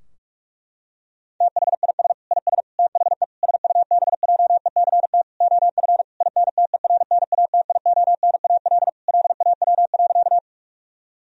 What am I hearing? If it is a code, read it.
Morse code, heard as a keyed tone, about 45 wpm, 700 Hz. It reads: THIS IS THE SUBJECT OF INTERNATIONAL LAW.
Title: Sentences from Top 1000 Words - Review Part 4 - 45wpm